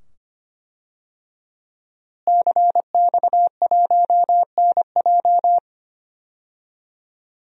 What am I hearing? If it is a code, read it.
Morse code, heard as a keyed tone, about 25 wpm, 700 Hz. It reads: CX1NJ
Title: DXCC-100 Call Signs - 5 Characters - Repeated 2x - 25wpm